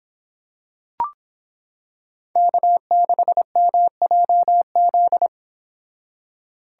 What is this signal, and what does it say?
Morse code, heard as a keyed tone, about 25 wpm, 700 Hz. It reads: K6MJZ